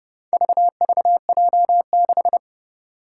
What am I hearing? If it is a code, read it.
Morse code, heard as a keyed tone, about 30 wpm, 700 Hz. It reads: VVJ6
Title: Four Challenging Letter-Number Combinations 30wpm